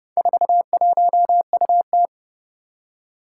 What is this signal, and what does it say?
Morse code, heard as a keyed tone, about 30 wpm, 700 Hz. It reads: 41UT
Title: Four Challenging Letter-Number Combinations 30wpm